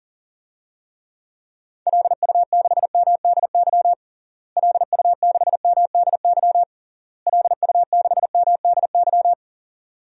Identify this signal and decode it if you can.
Morse code, heard as a keyed tone, about 40 wpm, 700 Hz. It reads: LU6MDY LU6MDY LU6MDY
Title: DXCC-100 Call Signs - 6 Characters - Repeated 3x - 40wpm